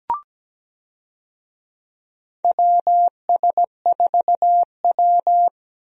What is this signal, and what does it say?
Morse code, heard as a keyed tone, about 17 wpm, 700 Hz. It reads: WS4W
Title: USA Call Signs - 4 Characters - 17wpm